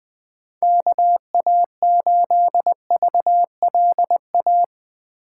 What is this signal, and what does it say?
Morse code, heard as a keyed tone, about 20 wpm, 700 Hz. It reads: KA8VLA
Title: USA Call Signs - 6 Characters - 4x Character Spacing - 20wpm